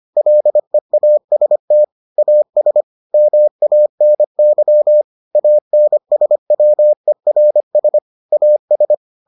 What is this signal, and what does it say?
Morse code, heard as a keyed tone, about 25 wpm, 600 Hz. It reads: LEAST AS MANY ANSWERS AS